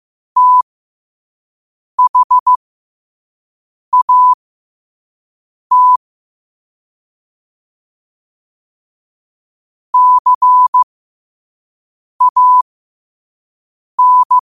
Morse code, heard as a keyed tone, about 15 wpm, spaced as for 5 wpm, 1000 Hz. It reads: THAT CAN
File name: qotdfiles / 5wpm